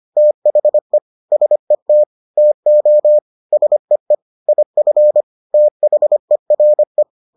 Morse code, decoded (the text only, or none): THE SET TO SEE IF THERE